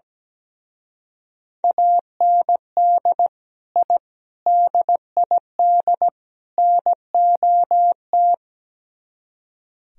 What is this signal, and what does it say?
Morse code, heard as a keyed tone, about 19 wpm, 700 Hz. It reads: AND I DID NOT
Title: Speed-Racing - Sets of 4 Words 17wpm